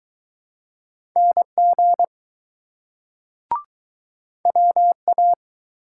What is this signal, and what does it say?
Morse code, heard as a keyed tone, about 21 wpm, 700 Hz. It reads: NG WA